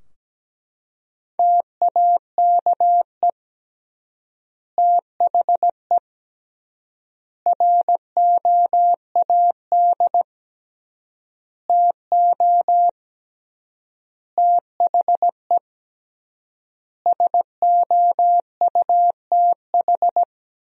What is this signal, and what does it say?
Morse code, heard as a keyed tone, about 17 wpm, 700 Hz. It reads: TAKE THE ROAD TO THE SOUTH
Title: 3x Word-Spacing - Sentences from Top 400 Words - 17wpm